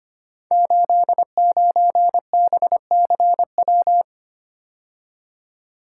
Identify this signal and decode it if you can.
Morse code, heard as a keyed tone, about 25 wpm, 700 Hz. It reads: 89BCW
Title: Code Groups - 5 Letter:Numbers - 25wpm